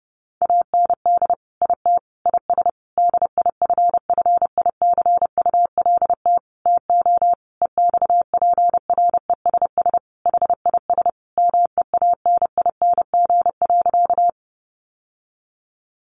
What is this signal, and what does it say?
Morse code, heard as a keyed tone, about 30 wpm, 700 Hz. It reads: AND IT IS DIFFICULT TO EXPRESS HIS MEANING.
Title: Sentences from Top 1100 Words 30wpm